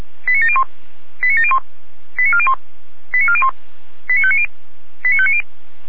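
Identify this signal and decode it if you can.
Five-tone selective call (ZVEI), heard as quick runs of five tones, tones of about 70 ms. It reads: 89701 89701 89401 89401 89590 89590
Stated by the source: ZVEI five-tone selective call received with an SDR.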